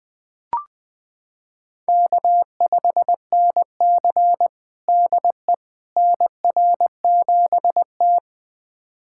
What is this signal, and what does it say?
Morse code, heard as a keyed tone, about 20 wpm, 700 Hz. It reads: K5NC DE NR7T
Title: USA Callsign DE Callsign - 4 Characters - 20wpm